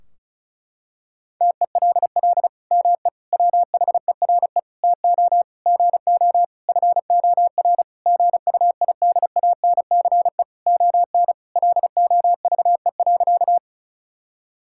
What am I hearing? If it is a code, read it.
Morse code, heard as a keyed tone, about 34 wpm, 700 Hz. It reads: TELL ME WHERE TO GO FOR GUIDANCE ON LOVE.
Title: Sentences from Top 1600 Words 35wpm